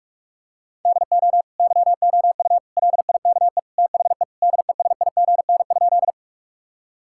Morse code, heard as a keyed tone, about 46 wpm, 700 Hz. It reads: DO YOU LIKE THE DESIGN?
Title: Sentences from Top 1000 Words - Review Part 5 - 45wpm